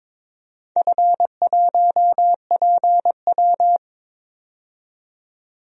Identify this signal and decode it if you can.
Morse code, heard as a keyed tone, about 22 wpm, 700 Hz. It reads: F1PW
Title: DXCC-100 Call Signs - 4 Characters - 22wpm